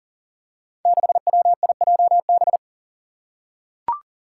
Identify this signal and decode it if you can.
Morse code, heard as a keyed tone, about 40 wpm, 700 Hz. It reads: 6WIJB
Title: Five Challenging Letter-Number Combinations 40wpm